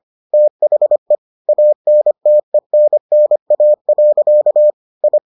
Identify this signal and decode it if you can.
Morse code, heard as a keyed tone, about 25 wpm, 600 Hz. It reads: THE ANTENNA. I